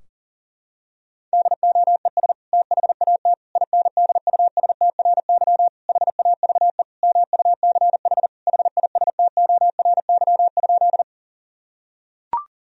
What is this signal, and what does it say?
Morse code, heard as a keyed tone, about 40 wpm, 700 Hz. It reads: DOES THAT INDUSTRY HAVE MUCH HISTORY?
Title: Sentences from Top 1000 Words - Review Part 4 - 40wpm